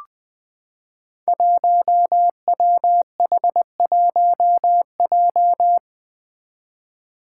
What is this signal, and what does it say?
Morse code, heard as a keyed tone, about 20 wpm, 700 Hz. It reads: 1WH1J
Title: Five Challenging Letter-Number Combinations 20wpm